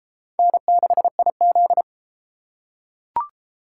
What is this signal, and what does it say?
Morse code, heard as a keyed tone, about 32 wpm, 700 Hz. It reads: N6IZ